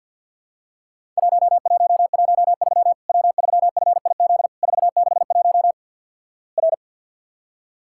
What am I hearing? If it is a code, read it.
Morse code, heard as a keyed tone, about 50 wpm, 700 Hz. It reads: 1113 W3FIZ 461 R
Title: Worked All Europe DX Contest - QTC Exchange - 50wpm